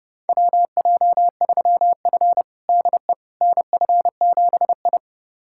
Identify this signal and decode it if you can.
Morse code, heard as a keyed tone, about 32 wpm, 700 Hz. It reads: WJ3F DE NF7I
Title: USA Callsign DE Callsign - 4 Characters - 30wpm